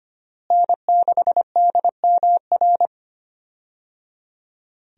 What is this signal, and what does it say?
Morse code, heard as a keyed tone, about 25 wpm, 700 Hz. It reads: N6DMR
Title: USA Call Signs - 5 Characters - Repeated 3x - 25wpm